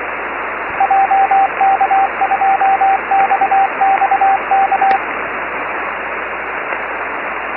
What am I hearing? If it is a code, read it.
Morse code, heard as a keyed tone, about 22 wpm, 780 Hz, with radio noise. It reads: JK2XXK
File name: FO29 9V1XX-d299c